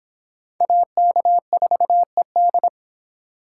Morse code, heard as a keyed tone, about 23 wpm, 700 Hz. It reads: AK4ED